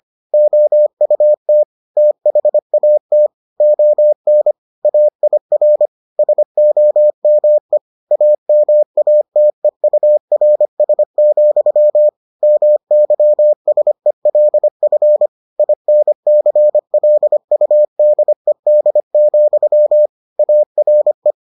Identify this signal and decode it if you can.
Morse code, heard as a keyed tone, about 25 wpm, 600 Hz. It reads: OUT THAT ON AIR SOME AMATEURS, MYSELF INCLUDED, ARE